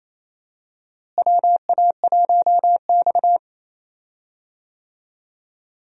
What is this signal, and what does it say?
Morse code, heard as a keyed tone, about 28 wpm, 700 Hz. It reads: WA1X
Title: USA Call Signs - 4 Characters - Repeated 2x - 28wpm